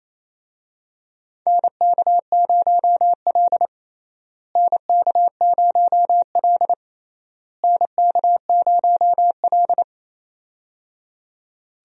Morse code, heard as a keyed tone, about 28 wpm, 700 Hz. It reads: NK0L NK0L NK0L
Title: USA Call Signs - 4 Characters - Repeated 3x - 28wpm